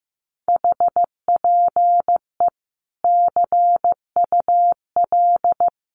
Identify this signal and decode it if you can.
Morse code, heard as a keyed tone, about 14 wpm, 700 Hz. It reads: HPE CUL
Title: Common QSO Phrases 15wpm